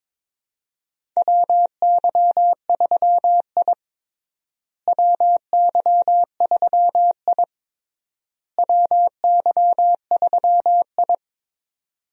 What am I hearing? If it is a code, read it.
Morse code, heard as a keyed tone, about 22 wpm, 700 Hz. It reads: WY3I WY3I WY3I
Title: USA Call Signs - 4 Characters - Repeated 3x - 22wpm